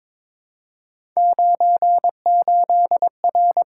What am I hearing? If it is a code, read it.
Morse code, heard as a keyed tone, about 23 wpm, 700 Hz. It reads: 98R
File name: Three Letter-Number Combinations - 22wpm